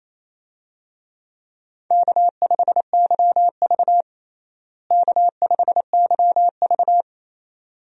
Morse code, heard as a keyed tone, about 27 wpm, 700 Hz. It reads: K5YV K5YV
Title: USA Call Signs - 4 Characters - Repeated 2x - 28wpm